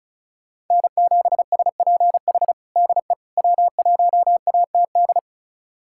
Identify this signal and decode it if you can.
Morse code, heard as a keyed tone, about 36 wpm, 700 Hz. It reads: N7SPH DE W1ATD
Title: USA Callsign DE Callsign - 5 Characters - 35wpm